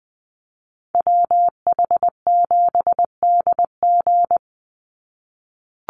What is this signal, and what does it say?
Morse code, heard as a keyed tone, about 20 wpm, 700 Hz. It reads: WH7DG
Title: USA Call Signs 20wpm